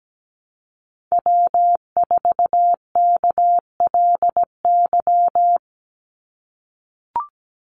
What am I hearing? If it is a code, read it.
Morse code, heard as a keyed tone, about 17 wpm, 700 Hz. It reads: W4KLY
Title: Call Signs - 17wpm